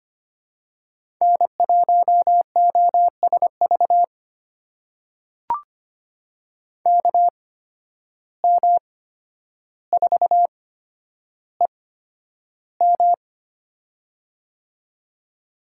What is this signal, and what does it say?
Morse code, heard as a keyed tone, about 25 wpm, spaced as for 11 wpm, 700 Hz. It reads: N1OSV KM4EM
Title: USA Call Signs - 5 Characters - 8x Character Spacing - 25wpm